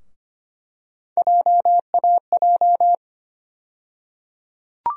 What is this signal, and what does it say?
Morse code, heard as a keyed tone, about 25 wpm, 700 Hz. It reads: JAJ
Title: USA Call Sign Postfixes - 3 Letters 25wpm